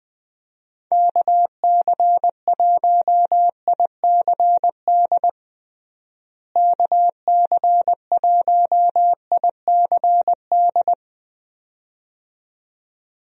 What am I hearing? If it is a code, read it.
Morse code, heard as a keyed tone, about 20 wpm, 700 Hz. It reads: KC1ICD KC1ICD
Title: USA Call Signs - 6 Characters - Repeated 2x - 20wpm